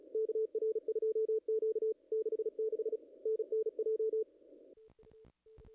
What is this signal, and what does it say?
Morse code, heard as a keyed tone, about 36 wpm, 440 Hz. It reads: KR2Q 66 NNJ
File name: contests / ss-cw-k5tr-op-n5rz -0000